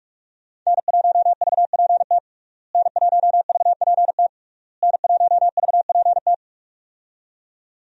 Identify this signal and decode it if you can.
Morse code, heard as a keyed tone, about 46 wpm, 700 Hz. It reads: N1VPT N1VPT N1VPT
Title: USA Call Signs - 5 Characters - Repeated 3x - 45wpm